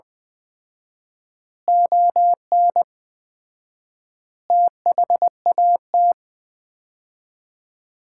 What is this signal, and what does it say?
Morse code, heard as a keyed tone, about 21 wpm, 700 Hz. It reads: ON THAT
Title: Common QSO Phrases - 4x Word-Spacing - 20wpm